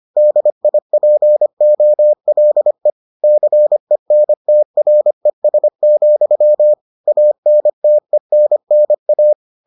Morse code, heard as a keyed tone, about 25 wpm, 600 Hz. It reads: DIPOLE CENTRES, ANTENNA